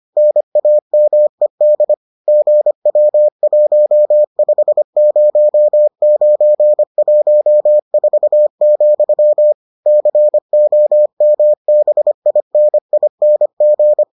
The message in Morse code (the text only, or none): NAMED GW150914, COMBINING